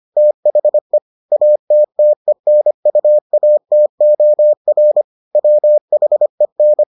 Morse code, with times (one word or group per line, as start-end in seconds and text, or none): 0.2-1.0 THE
1.3-5.0 ATTENUATOR
5.4-6.8 WHEN